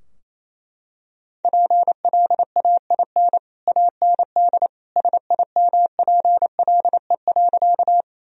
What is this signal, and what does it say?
Morse code, heard as a keyed tone, about 28 wpm, 700 Hz. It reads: PLAIN AND SIMPLE.
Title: Sentences from Top 1000 Words - Review Part 2 - 28wpm